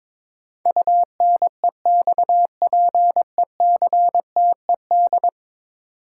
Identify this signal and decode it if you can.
Morse code, heard as a keyed tone, about 22 wpm, 700 Hz. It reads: UNEXPECTED
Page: morse-code-ninja-practice-22wpm